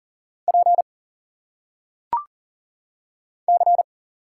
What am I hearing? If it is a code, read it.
Morse code, heard as a keyed tone, about 40 wpm, 700 Hz. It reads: P C
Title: Single Letters - Phonetic Pronunciation 40wpm